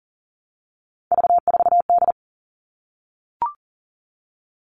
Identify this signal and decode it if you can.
Morse code, heard as a keyed tone, about 40 wpm, 700 Hz. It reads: V4D
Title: All Cut Numbers - 3-Digit 40wpm